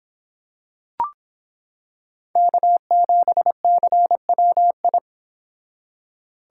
Morse code, heard as a keyed tone, about 26 wpm, 700 Hz. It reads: K7CWI